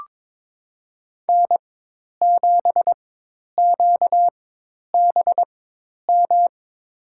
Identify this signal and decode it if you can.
Morse code, heard as a keyed tone, about 22 wpm, spaced as for 10 wpm, 700 Hz. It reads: N7QBM
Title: USA Call Signs - 5 Characters - 4x Character Spacing - 22wpm